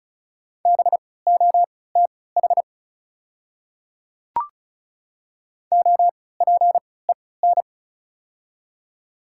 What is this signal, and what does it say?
Morse code, heard as a keyed tone, about 36 wpm, spaced as for 20 wpm, 700 Hz. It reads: BOTH OPEN
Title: 3x Character Spacing - Top 500 Words - 35wpm